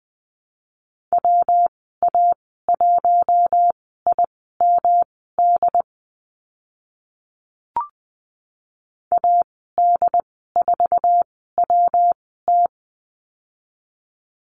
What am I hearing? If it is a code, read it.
Morse code, heard as a keyed tone, about 20 wpm, spaced as for 14 wpm, 700 Hz. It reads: WA1IMD AD4WT
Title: 128 Lesson 21g - Number 6 - Call Signs 20wpm